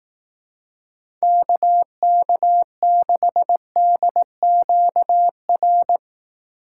Morse code, heard as a keyed tone, about 18 wpm, 700 Hz. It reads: KK6DQR
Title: Speed Racing - USA 2x3 Call Signs 15wpm